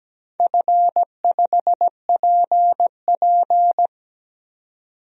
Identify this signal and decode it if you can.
Morse code, heard as a keyed tone, about 17 wpm, 700 Hz. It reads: F5PP
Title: DXCC-100 Call Signs - 4 Characters - 17wpm